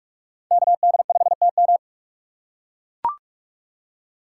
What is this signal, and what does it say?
Morse code, heard as a keyed tone, about 40 wpm, 700 Hz. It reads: KD5TM